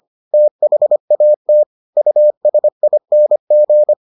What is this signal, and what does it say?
Morse code, heard as a keyed tone, about 25 wpm, 600 Hz. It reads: THAT USING